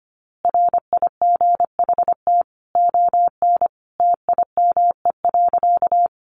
Morse code, heard as a keyed tone, about 25 wpm, 700 Hz. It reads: RIGHT ON TIME.
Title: A Sound Of Thunder 25wpm